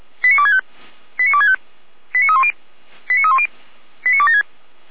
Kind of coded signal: five-tone selective call